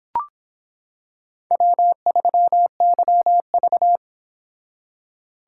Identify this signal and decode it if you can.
Morse code, heard as a keyed tone, about 25 wpm, 700 Hz. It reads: W3YV